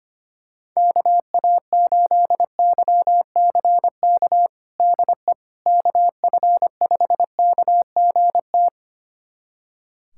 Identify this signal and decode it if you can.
Morse code, heard as a keyed tone, about 25 wpm, 700 Hz. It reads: KA8YCK DE KF5KGT
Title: USA Callsign DE Callsign - 6 Characters - 25wpm